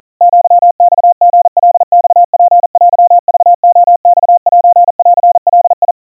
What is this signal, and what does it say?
Morse code (Japanese Wabun code), heard as a keyed tone, about 40 wpm, 700 Hz. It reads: アマリカマツテクレマセンガ
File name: kawa / Morse sample / kamyu 150 cpm 700 Hz Crispy